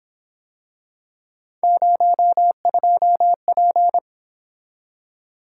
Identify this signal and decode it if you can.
Morse code, heard as a keyed tone, about 26 wpm, 700 Hz. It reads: 02P
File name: Speed-Racing - Three Letter-Number Combinations - 22wpm